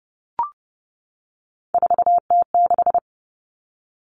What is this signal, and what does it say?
Morse code, heard as a keyed tone, about 30 wpm, 700 Hz. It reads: 4T6
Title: Common Cut Numbers - 3-Digit 30wpm